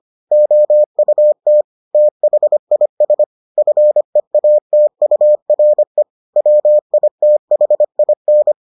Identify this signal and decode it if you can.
Morse code, heard as a keyed tone, about 25 wpm, 600 Hz. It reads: OUT THIS FEATURE WITHIN